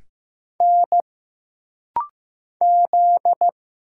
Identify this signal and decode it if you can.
Morse code, heard as a keyed tone, about 15 wpm, 700 Hz. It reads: N Z
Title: Single Letters - Rapid-Fire 15wpm